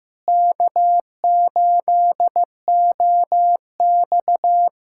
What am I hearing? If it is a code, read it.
Morse code, heard as a keyed tone, about 15 wpm, 700 Hz. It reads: K8OX
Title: USA Callsign DE Callsign - 4 Characters - 15wpm